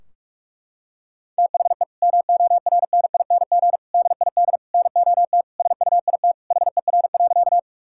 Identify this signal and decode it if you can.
Morse code, heard as a keyed tone, about 41 wpm, 700 Hz. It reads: THE MORNING DID NOT SUIT HER.
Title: Sentences from Top 1600 Words 45wpm